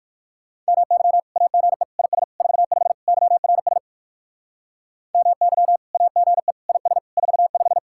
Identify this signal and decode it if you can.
Morse code, heard as a keyed tone, about 56 wpm, 700 Hz. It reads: MY AGE IS 45 YRS MY AGE IS 45
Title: Speed-Racing Common QSO Phrases Encore - 45wpm